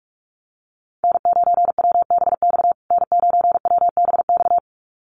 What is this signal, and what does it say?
Morse code, heard as a keyed tone, about 44 wpm, 700 Hz. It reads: N9WBX N9WBX
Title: USA Call Signs Repeated 2x 45wpm